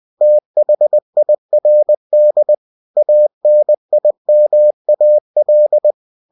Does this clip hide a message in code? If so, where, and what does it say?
Yes: Morse code from 0.2 to 5.9 s, heard as a keyed tone, about 20 wpm, 600 Hz. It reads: THIRD ANIMAL